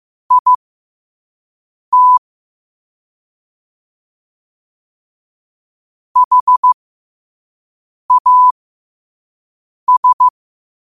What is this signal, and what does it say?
Morse code, heard as a keyed tone, about 15 wpm, spaced as for 5 wpm, 1000 Hz. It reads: IT HAS